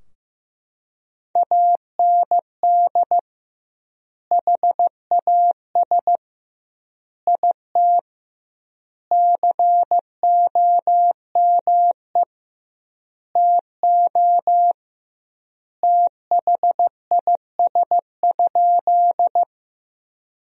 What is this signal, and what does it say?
Morse code, heard as a keyed tone, about 15 wpm, 700 Hz. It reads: AND HAS IT COME TO THIS?
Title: 2x Word-Spacing - Sentences from Top 100 Words - 15wpm